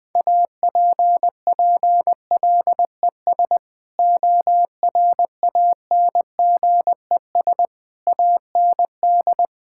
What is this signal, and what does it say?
Morse code, heard as a keyed tone, about 21 wpm, 700 Hz. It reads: APPLES ORANGES AND
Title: Sentences with 6-Word Phrases - 20wpm